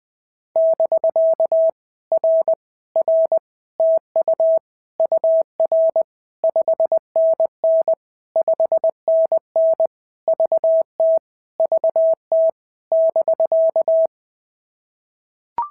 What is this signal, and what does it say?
Morse code, heard as a keyed tone, about 18 wpm, 650 Hz. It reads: <BK> R R TU UR 5NN 5NN VT VT <BK>
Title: POTA QSO Practice - 20wpm